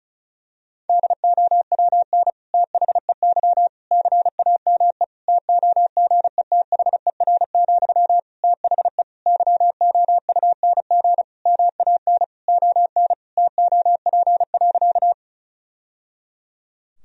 Morse code, heard as a keyed tone, about 36 wpm, 700 Hz. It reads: DOWN THEY CAME TOGETHER, THE YOUNG MAN ON TOP.
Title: Sentences from Top 1000 Words - Review Part 2 - 35wpm